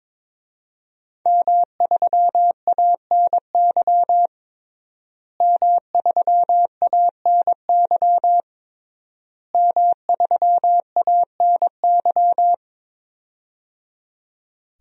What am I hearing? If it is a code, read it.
Morse code, heard as a keyed tone, about 22 wpm, 700 Hz. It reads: M3ANY M3ANY M3ANY
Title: DXCC-100 Call Signs - 5 Characters - Repeated 3x - 22wpm